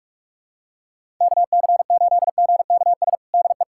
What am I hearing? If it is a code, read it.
Morse code, heard as a keyed tone, about 46 wpm, 700 Hz. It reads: KC8GKS DE
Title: USA Callsign DE Callsign 45wpm